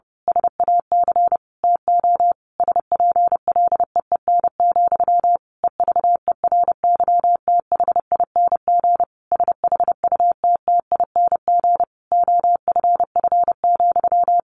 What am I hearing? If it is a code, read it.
Morse code, heard as a keyed tone, about 30 wpm, 700 Hz. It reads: SAC TO SPLEEN, EVERYTHING SHUTTING OFF,
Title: A Sound Of Thunder 30wpm